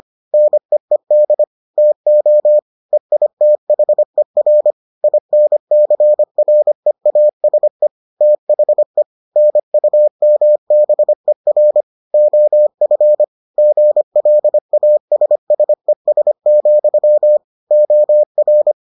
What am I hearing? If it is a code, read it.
Morse code, heard as a keyed tone, about 25 wpm, 600 Hz. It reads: NEED TO EITHER INCREASE THE NUMBER OF GLASSES, OR